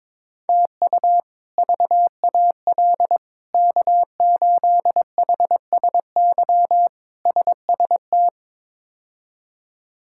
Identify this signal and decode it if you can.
Morse code, heard as a keyed tone, about 21 wpm, 700 Hz. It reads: TU VAL K8HSY SST
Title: K1USN SST QSO Practice - 22wpm